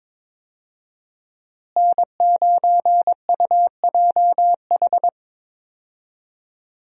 Morse code, heard as a keyed tone, about 22 wpm, 700 Hz. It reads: N9UJH
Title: USA Call Signs - 5 Characters - Repeated 3x - 22wpm